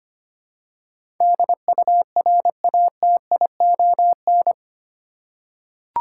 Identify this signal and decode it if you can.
Morse code, heard as a keyed tone, about 25 wpm, 700 Hz. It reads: DURATION
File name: 8 Letter Words - First 3 Letters Given - 25wpm